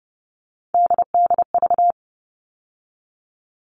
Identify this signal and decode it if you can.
Morse code, heard as a keyed tone, about 30 wpm, 700 Hz. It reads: DDV
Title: All Cut Numbers - 3-Digit 30wpm